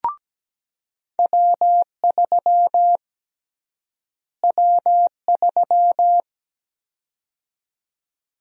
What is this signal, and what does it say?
Morse code, heard as a keyed tone, about 17 wpm, 700 Hz. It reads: W3 W3
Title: USA Call Sign Prefixes - Repeated 2x - 17wpm